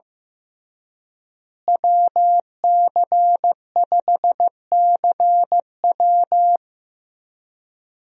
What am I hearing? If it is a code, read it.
Morse code, heard as a keyed tone, about 15 wpm, 700 Hz. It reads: WC5CW
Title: USA Call Signs - 5 Characters - Repeated 3x - 15wpm